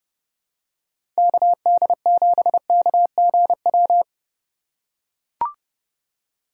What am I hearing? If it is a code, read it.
Morse code, heard as a keyed tone, about 32 wpm, 700 Hz. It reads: KD7KGW
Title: USA Call Signs - 6 Characters - 6x Character Spacing - 30wpm